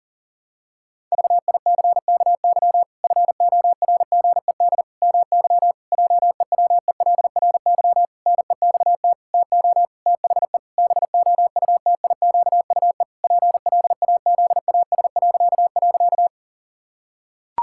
Morse code, heard as a keyed tone, about 40 wpm, 700 Hz. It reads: VICKY FORGED MY JEWELRY NEXT TO THE BOUTIQUE PLAZAS..
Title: Pangram Challenge v2 - 40wpm